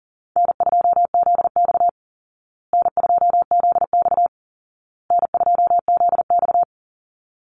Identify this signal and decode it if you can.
Morse code, heard as a keyed tone, about 40 wpm, 700 Hz. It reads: N2ZX N2ZX N2ZX
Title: Call Signs Repeated 2x - 40wpm